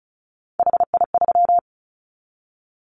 Morse code, heard as a keyed tone, about 34 wpm, 700 Hz. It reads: HI3
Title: Callsign Prefixes for Top 100 DXCC - 35wpm